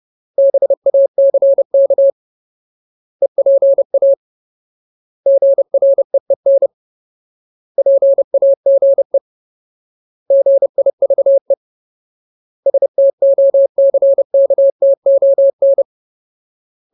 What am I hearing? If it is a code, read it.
Morse code, heard as a keyed tone, about 30 wpm, 550 Hz. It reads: BACK EPA GREEN PAGE GIVE STOCKTON
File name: Ultimate Word List / 30WPM Ultimate